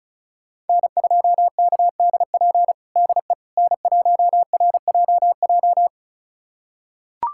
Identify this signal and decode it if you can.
Morse code, heard as a keyed tone, about 34 wpm, 700 Hz. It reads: N2KDP DE N1RJJ
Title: USA Callsign DE Callsign - 5 Characters - 35wpm